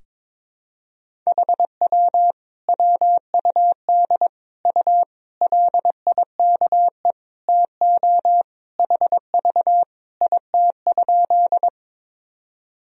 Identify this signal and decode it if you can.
Morse code, heard as a keyed tone, about 22 wpm, 700 Hz. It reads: HW WUD U LIKE TO HV IT?
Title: Sentences from Top 100 Words and Abbreviations 22wpm